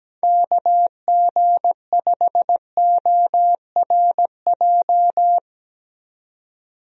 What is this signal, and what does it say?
Morse code, heard as a keyed tone, about 17 wpm, 700 Hz. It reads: KG5ORJ
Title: USA Call Signs - 6 Characters - 17wpm